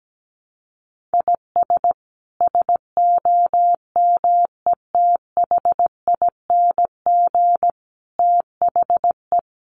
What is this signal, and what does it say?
Morse code, heard as a keyed tone, about 17 wpm, 700 Hz. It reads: IS SOMETHING THE
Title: Sentences from Top 700 Words 17wpm